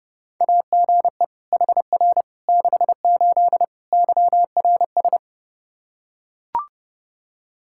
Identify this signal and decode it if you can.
Morse code, heard as a keyed tone, about 30 wpm, 700 Hz. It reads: AGE HR 68 YRS
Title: K1USN SST QSO Practice 30wpm